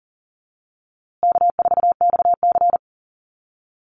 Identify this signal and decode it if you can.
Morse code, heard as a keyed tone, about 40 wpm, 700 Hz. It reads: K4XC
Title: USA Call Signs Repeated 3x 40wpm